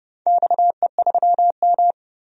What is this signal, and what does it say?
Morse code, heard as a keyed tone, about 29 wpm, 700 Hz. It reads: XE3M
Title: DXCC-100 Call Signs - 4 Characters - 30wpm